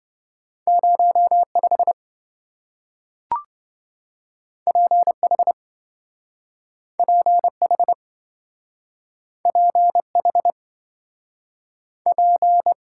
Morse code, heard as a keyed tone, about 27 wpm, 700 Hz. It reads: 05 PH PH PH P